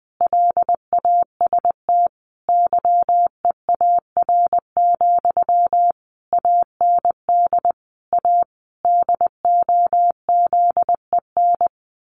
Morse code, heard as a keyed tone, about 20 wpm, 700 Hz. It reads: LAST YEAR, AND A DOZEN
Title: A Sound Of Thunder 20wpm